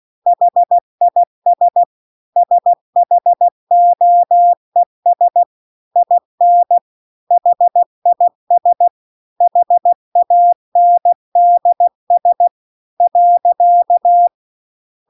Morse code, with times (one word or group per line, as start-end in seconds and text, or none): 0.3-1.8 HIS
2.4-5.4 SHOES
6.0-6.8 IN
7.3-8.9 HIS
9.4-12.5 HANDS
13.0-14.3 .